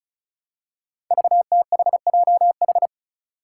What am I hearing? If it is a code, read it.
Morse code, heard as a keyed tone, about 36 wpm, 700 Hz. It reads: VTHJH
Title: Five Challenging Letter-Number Combinations 35wpm